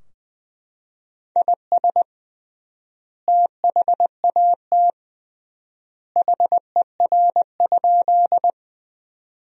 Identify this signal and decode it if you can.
Morse code, heard as a keyed tone, about 20 wpm, 700 Hz. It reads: IS THAT HER?
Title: 3x Word-Spacing - Sentences from Top 100 Words - 20wpm